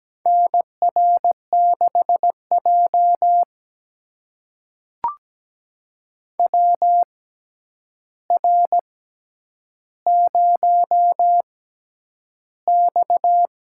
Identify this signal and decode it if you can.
Morse code, heard as a keyed tone, about 17 wpm, spaced as for 9 wpm, 700 Hz. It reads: NR6J WR0X
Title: USA Call Signs - 4 Characters - 6x Character Spacing - 17wpm